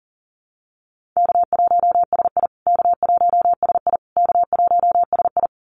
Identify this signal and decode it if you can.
Morse code, heard as a keyed tone, about 40 wpm, 700 Hz. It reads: K1SI K1SI K1SI
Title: USA Call Signs Repeated 3x 40wpm